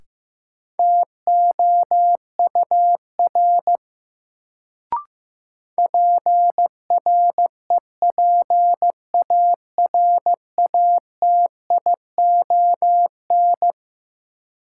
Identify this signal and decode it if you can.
Morse code, heard as a keyed tone, about 15 wpm, 700 Hz. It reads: TOUR PREPARATION
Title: Sentences from Top 1400 Words - New Word Drill Alt - 15wpm